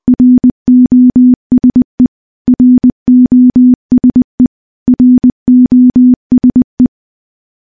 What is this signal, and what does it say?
Morse code, heard as a keyed tone, about 20 wpm, 250 Hz. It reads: ROSE ROSE ROSE
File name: morse / morse-training-20WPM